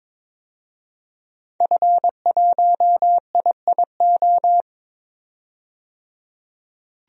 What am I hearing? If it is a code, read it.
Morse code, heard as a keyed tone, about 22 wpm, 700 Hz. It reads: F1IIO
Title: DXCC-100 Call Signs - 5 Characters - 22wpm